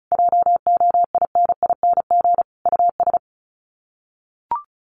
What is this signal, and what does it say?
Morse code, heard as a keyed tone, about 34 wpm, 700 Hz. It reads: JOINING US
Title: Sets of 5 Words 35wpm